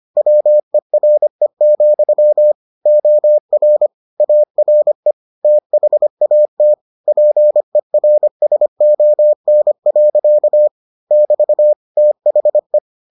Morse code, heard as a keyed tone, about 25 wpm, 600 Hz. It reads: WERE, OR ARE THAT PERSON. <BT> THE